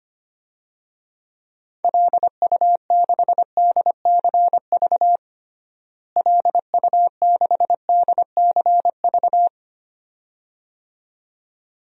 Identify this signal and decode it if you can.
Morse code, heard as a keyed tone, about 25 wpm, 700 Hz. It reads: LU6DCV LU6DCV
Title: DXCC-100 Call Signs - 6 Characters - Repeated 2x - 25wpm